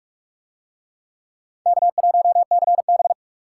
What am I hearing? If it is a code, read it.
Morse code, heard as a keyed tone, about 41 wpm, 700 Hz. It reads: K1CB